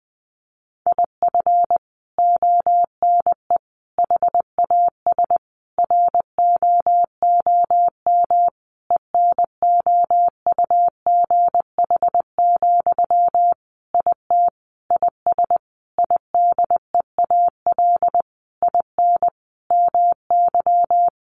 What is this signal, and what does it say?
Morse code, heard as a keyed tone, about 20 wpm, 700 Hz. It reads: IF ONE HAS ROOM ENOUGH, IT IS IDEAL IN MY
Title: Sentences from Top 1400 Words 20wpm